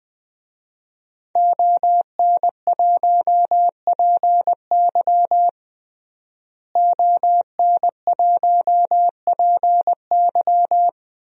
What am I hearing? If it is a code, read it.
Morse code, heard as a keyed tone, about 20 wpm, 700 Hz. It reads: ON1PY ON1PY
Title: DXCC-100 Call Signs - 5 Characters - Repeated 2x - 20wpm